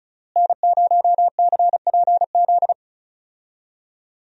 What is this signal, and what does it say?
Morse code, heard as a keyed tone, about 36 wpm, 700 Hz. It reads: N0CPZ
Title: USA Call Signs - 5 Characters - Repeated 3x - 35wpm